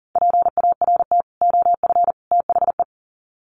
Morse code, heard as a keyed tone, about 40 wpm, 700 Hz. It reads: PART OF THE
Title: Sets of 3 Words 40wpm